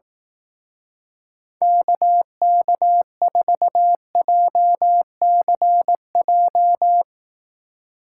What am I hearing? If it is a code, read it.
Morse code, heard as a keyed tone, about 18 wpm, 700 Hz. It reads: KK4JCJ
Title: Speed Racing - USA 2x3 Call Signs 15wpm